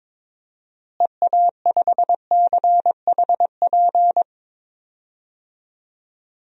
Morse code, heard as a keyed tone, about 22 wpm, 700 Hz. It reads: EA5CHP
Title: DXCC-100 Call Signs - 6 Characters - Repeated 2x - 22wpm